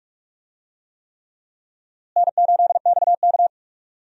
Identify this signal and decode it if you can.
Morse code, heard as a keyed tone, about 46 wpm, 700 Hz. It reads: N8XK
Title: USA Call Signs - 4 Characters - Repeated 3x - 45wpm